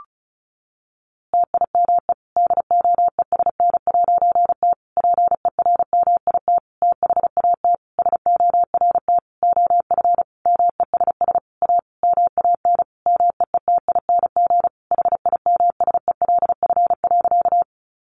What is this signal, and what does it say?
Morse code, heard as a keyed tone, about 34 wpm, 700 Hz. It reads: TIME DOESN'T PERMIT THAT SORT OF MESS A MAN MEETING HIMSELF.
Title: A Sound Of Thunder 35wpm